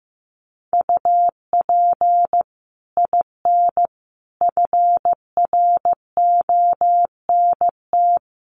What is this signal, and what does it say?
Morse code, heard as a keyed tone, about 15 wpm, 700 Hz. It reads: UP IN FRONT
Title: Sets of 5 Words Encore2 15wpm